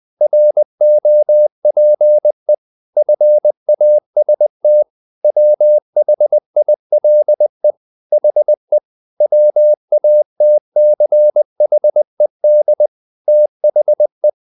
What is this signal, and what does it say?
Morse code, heard as a keyed tone, about 20 wpm, 600 Hz. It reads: ROPE FAST WHILE HE WATCHED THE
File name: the old man and the sea part 2 20wpm